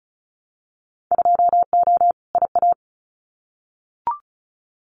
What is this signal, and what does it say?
Morse code, heard as a keyed tone, about 36 wpm, 690 Hz. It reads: 2O IA
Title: Winter Field Day Exchanges 35wpm